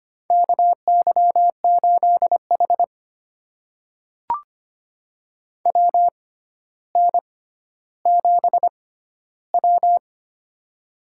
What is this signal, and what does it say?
Morse code, heard as a keyed tone, about 25 wpm, spaced as for 13 wpm, 700 Hz. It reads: KY8H WN7W
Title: USA Call Signs - 4 Characters - 6x Character Spacing - 25wpm